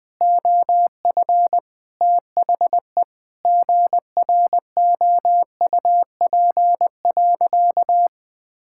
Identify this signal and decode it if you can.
Morse code, heard as a keyed tone, about 20 wpm, 700 Hz. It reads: OF THE GROUP.
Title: Sentences with 5-Word Phrases - Spoken Beginning - 20wpm